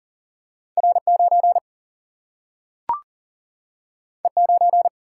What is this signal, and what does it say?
Morse code, heard as a keyed tone, about 40 wpm, 700 Hz. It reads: R9 E9
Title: Single-Letter Single-Number 40wpm